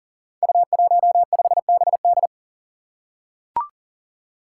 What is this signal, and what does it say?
Morse code, heard as a keyed tone, about 38 wpm, 690 Hz. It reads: U15BD
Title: Five Challenging Letter-Number Combinations 40wpm